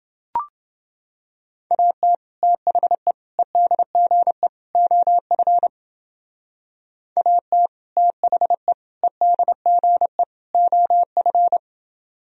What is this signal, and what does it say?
Morse code, heard as a keyed tone, about 28 wpm, 700 Hz. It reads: AT THE EDGE OF AT THE EDGE OF